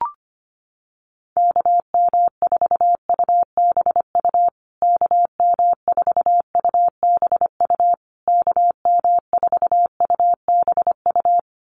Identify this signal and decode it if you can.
Morse code, heard as a keyed tone, about 25 wpm, 700 Hz. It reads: KM4UBU KM4UBU KM4UBU
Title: USA Call Signs Repeated 3x 25wpm